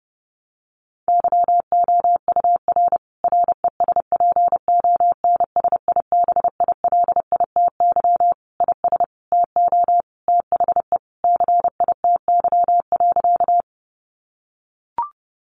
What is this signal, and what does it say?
Morse code, heard as a keyed tone, about 30 wpm, 700 Hz. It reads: YOUR RESPONSIBILITY IS TO THE CITY.
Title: Sentences from Top 1200 Words 30wpm